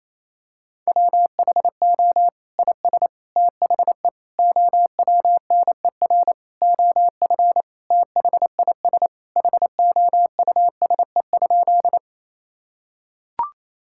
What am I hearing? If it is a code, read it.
Morse code, heard as a keyed tone, about 28 wpm, 700 Hz. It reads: WHO IS THE OWNER OF THIS HOUSE?
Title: Sentences from Top 1600 Words 28wpm